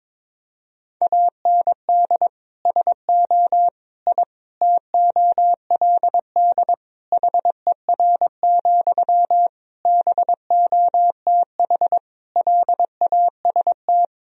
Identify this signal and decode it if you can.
Morse code, heard as a keyed tone, about 22 wpm, 700 Hz. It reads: AND SO I TOLD HER, BOTH LAST
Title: Sentences from Top 1000 Words - Review Part 4 - 22wpm